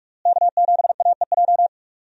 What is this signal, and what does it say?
Morse code, heard as a keyed tone, about 46 wpm, 700 Hz. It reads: K7AEJ
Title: USA Call Signs - 5 Characters - Repeated 2x - 45wpm